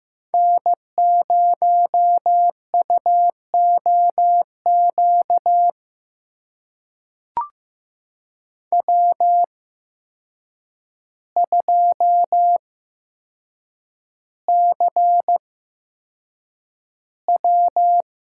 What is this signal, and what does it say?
Morse code, heard as a keyed tone, about 15 wpm, spaced as for 7 wpm, 700 Hz. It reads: N0UOQ W2CW
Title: USA Call Signs - 5 Characters - 8x Character Spacing - 15wpm